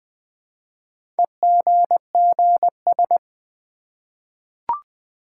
Morse code, heard as a keyed tone, about 20 wpm, 700 Hz. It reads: EGGS
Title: Easy Words with Hints - 20wpm